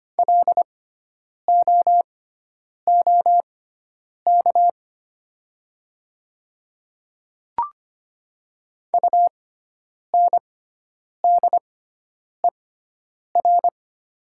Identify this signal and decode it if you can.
Morse code, heard as a keyed tone, about 25 wpm, spaced as for 9 wpm, 700 Hz. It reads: LOOK UNDER
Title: 6x Character Spacing - Top 200 Words - 25wpm